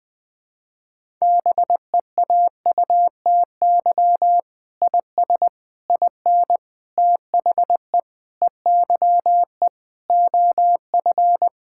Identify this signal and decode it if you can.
Morse code, heard as a keyed tone, about 21 wpm, 700 Hz. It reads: BEAUTY IS IN THE EYE OF
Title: Sentences with 7-Word Phrases - Spoken Beginning - 20wpm